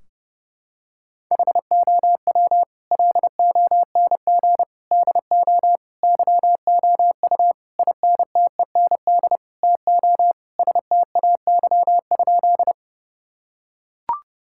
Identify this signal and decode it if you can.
Morse code, heard as a keyed tone, about 28 wpm, 700 Hz. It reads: HOW LONG DO YOU INTEND TO STAY?
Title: Sentences from Top 1600 Words 30wpm